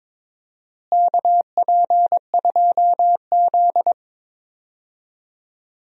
Morse code, heard as a keyed tone, about 22 wpm, 700 Hz. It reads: KP2Z
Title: DXCC-100 Call Signs - 4 Characters - Repeated 3x - 22wpm